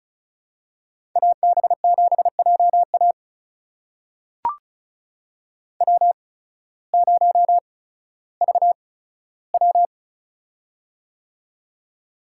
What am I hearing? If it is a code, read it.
Morse code, heard as a keyed tone, about 34 wpm, spaced as for 16 wpm, 700 Hz. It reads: AB7JA W0VW
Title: USA Call Signs - 8x Character Spacing - 35wpm